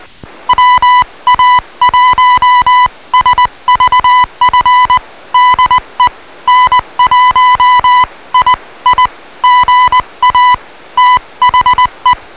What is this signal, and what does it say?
Morse code, heard as a keyed tone, about 20 wpm, 1000 Hz. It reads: WA1SVF DE N1II GA THE